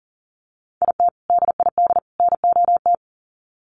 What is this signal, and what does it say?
Morse code, heard as a keyed tone, about 40 wpm, 700 Hz. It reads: IT DID NOT
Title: Sets of 3 Words 40wpm